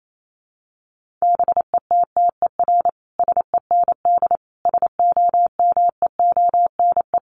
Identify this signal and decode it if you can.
Morse code, heard as a keyed tone, about 28 wpm, 700 Hz. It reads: BETTER SEND SOMEONE
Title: Sentences from Top 1500 Words 28wpm